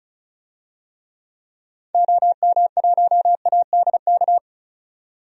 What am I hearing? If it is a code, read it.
Morse code, heard as a keyed tone, about 36 wpm, 700 Hz. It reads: OM1ADK
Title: DXCC-100 Call Signs - 6 Characters - 35wpm